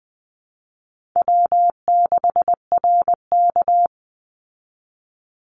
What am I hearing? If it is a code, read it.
Morse code, heard as a keyed tone, about 20 wpm, 700 Hz. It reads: W6RK
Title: Easy Call Signs Repeated 2x - 20wpm